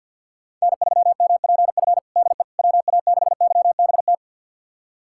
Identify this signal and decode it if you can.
Morse code, heard as a keyed tone, about 51 wpm, 690 Hz. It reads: N3MPF DE WA6YBT
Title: USA Callsign DE Callsign 50wpm